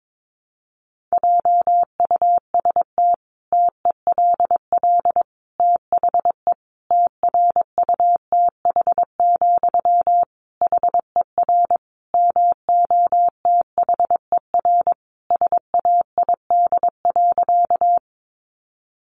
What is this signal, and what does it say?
Morse code, heard as a keyed tone, about 22 wpm, 700 Hz. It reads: JUST TELL THE TRUTH, HER MOTHER SAID.
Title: Sentences from Top 800 Words 22wpm